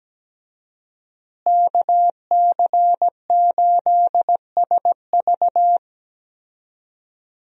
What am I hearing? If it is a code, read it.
Morse code, heard as a keyed tone, about 17 wpm, 700 Hz. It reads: KC8SV
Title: USA Call Signs - 5 Characters - Repeated 3x - 17wpm